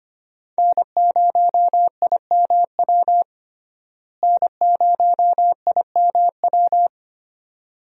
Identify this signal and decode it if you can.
Morse code, heard as a keyed tone, about 25 wpm, 700 Hz. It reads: N0IMW N0IMW
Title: USA Call Signs - 5 Characters - Repeated 2x - 25wpm